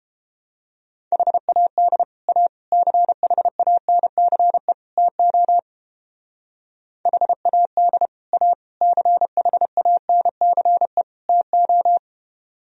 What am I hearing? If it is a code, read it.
Morse code, heard as a keyed tone, about 29 wpm, 700 Hz. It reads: HAD A CHANCE TO HAD A CHANCE TO